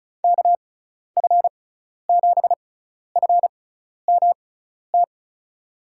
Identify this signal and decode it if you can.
Morse code, heard as a keyed tone, about 34 wpm, spaced as for 12 wpm, 700 Hz. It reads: KF7FMT
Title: USA Call Signs - 6 Characters - 6x Character Spacing - 35wpm